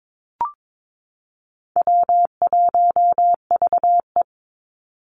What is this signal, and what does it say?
Morse code, heard as a keyed tone, about 22 wpm, 700 Hz. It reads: W1VE
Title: Easy Call Signs Repeated 2x - 22wpm